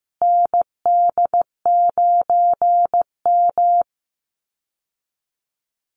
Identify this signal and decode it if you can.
Morse code, heard as a keyed tone, about 15 wpm, 700 Hz. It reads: ND9M
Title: Call Signs Repeated 2x - 15wpm